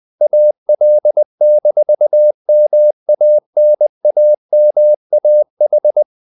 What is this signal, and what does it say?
Morse code, heard as a keyed tone, about 20 wpm, 600 Hz. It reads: AL-MANAMAH
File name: CW / Practice / NationalCapitals20